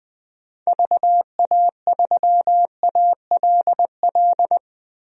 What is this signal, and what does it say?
Morse code, heard as a keyed tone, about 20 wpm, 700 Hz. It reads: VA3ALL
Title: DXCC-100 Call Signs - 6 Characters - Repeated 2x - 20wpm